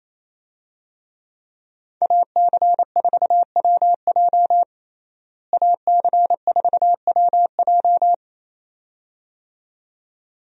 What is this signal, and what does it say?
Morse code, heard as a keyed tone, about 28 wpm, 700 Hz. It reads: AC4WJ AC4WJ
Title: USA Call Signs - 5 Characters - Repeated 2x - 28wpm